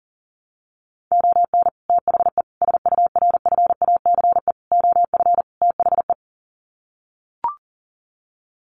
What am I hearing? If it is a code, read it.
Morse code, heard as a keyed tone, about 40 wpm, 700 Hz. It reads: ON THE SURFACE OF THE
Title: Sets of 5 Words Encore 40wpm